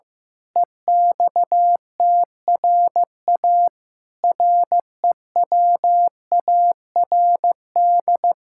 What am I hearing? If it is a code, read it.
Morse code, heard as a keyed tone, about 15 wpm, 700 Hz. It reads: EXTRA REWARD
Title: Pangram Challenge v2 - 15wpm